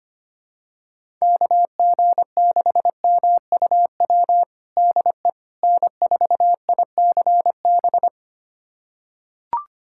Morse code, heard as a keyed tone, about 25 wpm, 700 Hz. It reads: KG6MUW DE N4ICB
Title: USA Callsign DE Callsign 25wpm